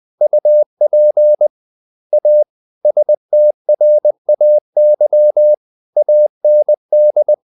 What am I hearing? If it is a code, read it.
Morse code, heard as a keyed tone, about 20 wpm, 600 Hz. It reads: UP A STRAY AND
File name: the old man and the sea / the old man and the sea part 1 20wpm